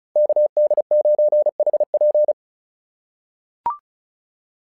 Morse code, heard as a keyed tone, about 36 wpm, 660 Hz. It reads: KD9HP
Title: SOTA QSO Practice 35wpm